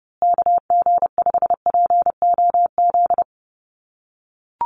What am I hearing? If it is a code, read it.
Morse code, heard as a keyed tone, about 30 wpm, 700 Hz. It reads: KG5POZ
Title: USA Call Signs Repeated 2x 30wpm